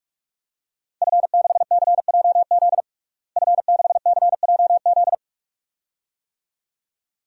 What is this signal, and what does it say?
Morse code, heard as a keyed tone, about 46 wpm, 700 Hz. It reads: F6CJZ F6CJZ
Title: DXCC-100 Call Signs - 5 Characters - Repeated 2x - 45wpm